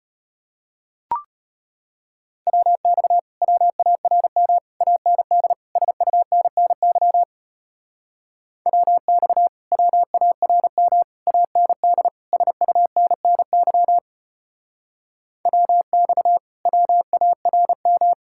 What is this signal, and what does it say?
Morse code, heard as a keyed tone, about 33 wpm, 700 Hz. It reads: WX WARM AND SUNNY WX WARM AND SUNNY WX WARM